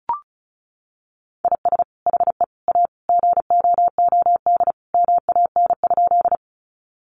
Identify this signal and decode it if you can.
Morse code, heard as a keyed tone, about 34 wpm, 700 Hz. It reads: IS HE A GOOD MAN?
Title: Sentences from Top 200 Words 35wpm